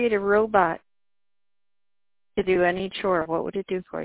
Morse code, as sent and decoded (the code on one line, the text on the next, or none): none